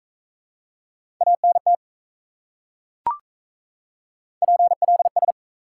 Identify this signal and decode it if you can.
Morse code, heard as a keyed tone, about 37 wpm, 700 Hz. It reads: ANT PLS